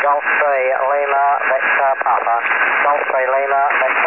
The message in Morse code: none